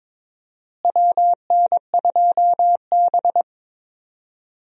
Morse code, heard as a keyed tone, about 22 wpm, 700 Hz. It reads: WN2B
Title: USA Call Signs - 4 Characters - Repeated 3x - 22wpm